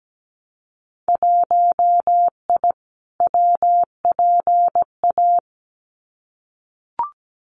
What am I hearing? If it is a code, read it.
Morse code, heard as a keyed tone, about 17 wpm, 700 Hz. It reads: 1I WPA
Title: Winter Field Day Exchanges Encore 17wpm